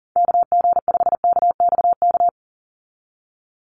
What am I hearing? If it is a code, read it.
Morse code, heard as a keyed tone, about 40 wpm, 700 Hz. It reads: KG5KXK
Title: USA Call Signs 40wpm